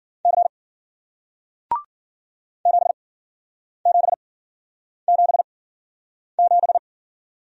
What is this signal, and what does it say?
Morse code, heard as a keyed tone, about 53 wpm, spaced as for 9 wpm, 700 Hz. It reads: C 7777